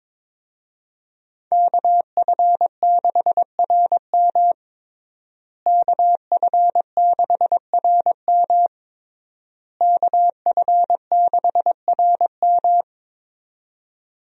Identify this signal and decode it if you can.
Morse code, heard as a keyed tone, about 21 wpm, 700 Hz. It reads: KF6RM KF6RM KF6RM
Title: USA Call Signs - 5 Characters - Repeated 3x - 22wpm